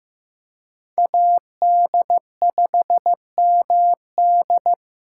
Morse code, heard as a keyed tone, about 18 wpm, 700 Hz. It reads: AD5MD
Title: Speed Racing - USA 2x2 Call Signs 15wpm